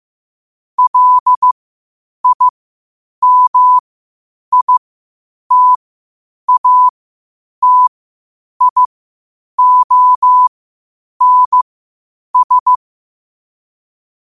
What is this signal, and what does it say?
Morse code, heard as a keyed tone, about 15 wpm, spaced as for 8 wpm, 1000 Hz. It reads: LIMITATIONS
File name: qotdfiles / 8wpm